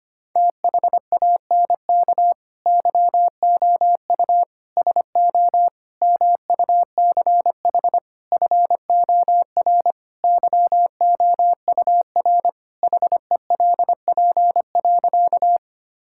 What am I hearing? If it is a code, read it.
Morse code, heard as a keyed tone, about 25 wpm, 700 Hz. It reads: THANK YOU SO MUCH FOR YOUR HELP.
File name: practice / 25wpm / Sentences with 4-Word Phrases - 25wpm